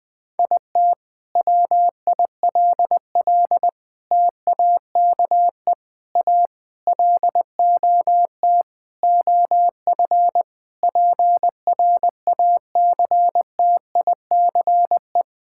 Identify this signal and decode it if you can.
Morse code, heard as a keyed tone, about 20 wpm, 700 Hz. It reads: IT WILL TAKE A LOT OF PRACTICE
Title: Sentences from Top 1000 Words - Review Part 6 - 20wpm